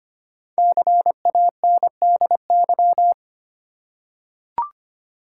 Morse code, heard as a keyed tone, about 25 wpm, 700 Hz. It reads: CANDY
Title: Five Letter Words - 25wpm